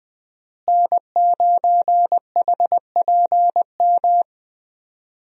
Morse code, heard as a keyed tone, about 20 wpm, 700 Hz. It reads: N9HPM
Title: USA Call Signs - 5 Characters - Repeated 3x - 20wpm